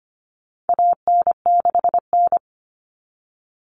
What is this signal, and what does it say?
Morse code, heard as a keyed tone, about 25 wpm, 700 Hz. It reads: AN6N
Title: Common Cut Numbers - 4-Digit 25wpm